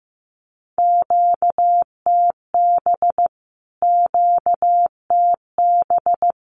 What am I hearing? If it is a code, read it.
Morse code, heard as a keyed tone, about 15 wpm, 700 Hz. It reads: QTB QTB
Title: Q-Codes Repeated 3x 15wpm